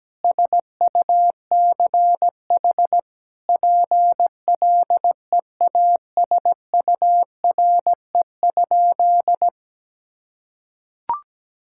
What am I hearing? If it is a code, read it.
Morse code, heard as a keyed tone, about 17 wpm, 700 Hz. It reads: SUCH PLEASURE?
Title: Sentences from Top 1600 Words 17wpm